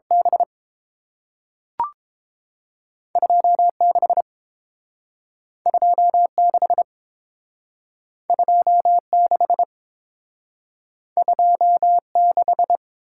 Morse code, heard as a keyed tone, about 30 wpm, 700 Hz. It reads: B 26 26 26 26